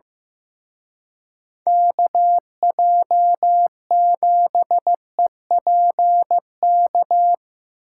Morse code, heard as a keyed tone, about 17 wpm, 700 Hz. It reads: KJ7EPK
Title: Speed Racing - USA 2x3 Call Signs 15wpm